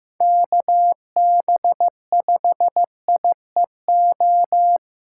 Morse code, heard as a keyed tone, about 15 wpm, 700 Hz. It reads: KB5IEO
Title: USA Callsign DE Callsign 15wpm